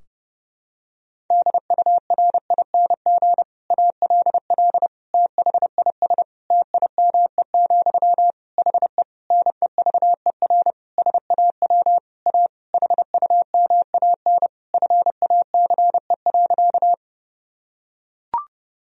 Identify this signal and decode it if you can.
Morse code, heard as a keyed tone, about 30 wpm, 700 Hz. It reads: DURING ALL THIS TIME, HE NEVER SAW A HUMAN FACE.
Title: Sentences from Top 1000 Words - Review Part 3 - 30wpm